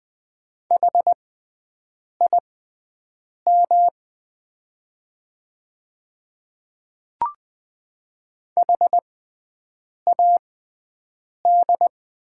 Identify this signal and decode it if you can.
Morse code, heard as a keyed tone, about 20 wpm, spaced as for 7 wpm, 700 Hz. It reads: HIM HAD
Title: 6x Character Spacing - Top 100 Words - 20wpm